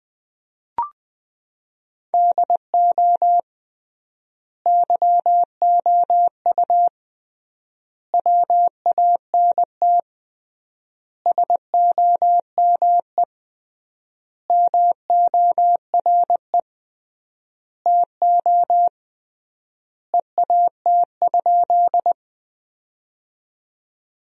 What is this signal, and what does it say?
Morse code, heard as a keyed tone, about 20 wpm, 700 Hz. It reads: DO YOU WANT SOME MORE TO EAT?
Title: 3x Word-Spacing - Sentences from Top 300 Words - 20wpm